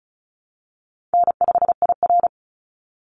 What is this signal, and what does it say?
Morse code, heard as a keyed tone, about 34 wpm, 700 Hz. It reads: N5IR
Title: Call Signs Repeated 2x - 35wpm